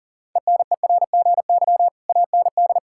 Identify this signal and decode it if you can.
Morse code, heard as a keyed tone, about 40 wpm, 700 Hz. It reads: ENERGY AND
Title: Sentences from Top 1000 Words - Review Part 6 - 40wpm